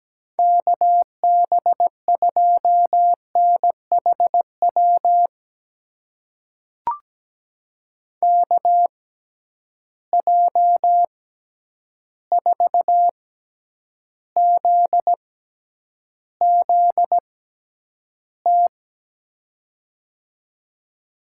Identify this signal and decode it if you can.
Morse code, heard as a keyed tone, about 17 wpm, spaced as for 9 wpm, 700 Hz. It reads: KB2NHW KJ4ZZT
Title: USA Call Signs - 6 Characters - 6x Character Spacing - 17wpm